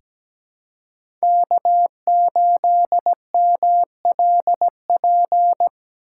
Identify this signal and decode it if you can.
Morse code, heard as a keyed tone, about 17 wpm, 700 Hz. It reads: K8MLP
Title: USA Call Signs - 5 Characters - 6x Character Spacing - 17wpm